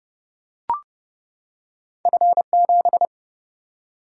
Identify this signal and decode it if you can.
Morse code, heard as a keyed tone, about 30 wpm, 700 Hz. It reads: F7
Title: Single-Letter Single-Number 30wpm